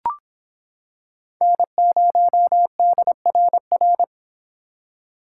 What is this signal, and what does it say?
Morse code, heard as a keyed tone, about 26 wpm, 700 Hz. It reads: N0DRR